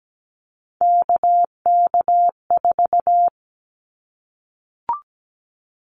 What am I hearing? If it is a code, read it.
Morse code, heard as a keyed tone, about 17 wpm, 700 Hz. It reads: KK4
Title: Callsign Prefixes for Top 25 DXCC - 17wpm